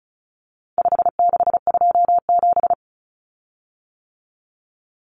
Morse code, heard as a keyed tone, about 34 wpm, 700 Hz. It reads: 5627
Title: Common Cut Numbers - 4-Digit 35wpm